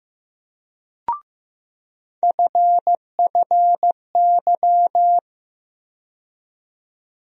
Morse code, heard as a keyed tone, about 15 wpm, 700 Hz. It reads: FFY
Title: USA Call Sign Postfixes - 3 Letters 15wpm